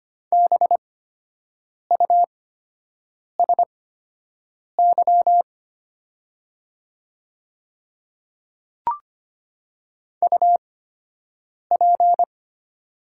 Morse code, heard as a keyed tone, about 25 wpm, spaced as for 7 wpm, 700 Hz. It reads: BUSY UP
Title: 8x Character Spacing - Top 500 Words - 25wpm